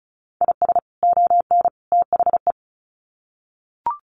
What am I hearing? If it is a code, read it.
Morse code, heard as a keyed tone, about 34 wpm, 700 Hz. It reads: IS ON THE
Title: Sets of 3 Words 35wpm